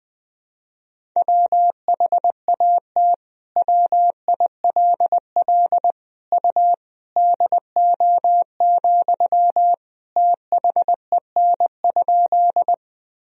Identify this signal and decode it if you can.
Morse code, heard as a keyed tone, about 21 wpm, 700 Hz. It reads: WHAT WILL U DO, THEN?
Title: Sentences from Top 100 Words and Abbreviations 20wpm